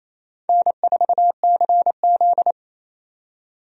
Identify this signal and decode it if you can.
Morse code, heard as a keyed tone, about 28 wpm, 700 Hz. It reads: N4CZ
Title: USA Call Signs - 4 Characters - 28wpm